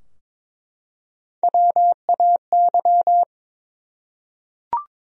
Morse code, heard as a keyed tone, about 22 wpm, 700 Hz. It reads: WAY
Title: Top 100 Words Repeated 22wpm